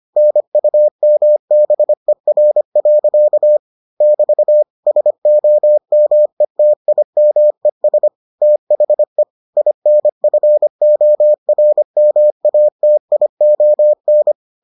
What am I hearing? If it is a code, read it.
Morse code, heard as a keyed tone, about 25 wpm, 600 Hz. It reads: NUMBER. <BT> SOMETIMES THE INFORMATION